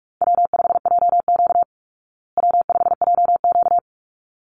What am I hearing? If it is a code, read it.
Morse code, heard as a keyed tone, about 46 wpm, 700 Hz. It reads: W5JQ W5JQ
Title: Call Signs Repeated 3x - 45wpm